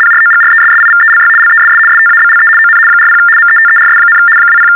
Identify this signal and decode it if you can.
RTTY, about 75 baud. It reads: PROGRESSIVELY LESS STABLE ON 40 METERS, 20
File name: teletype-RTTY-75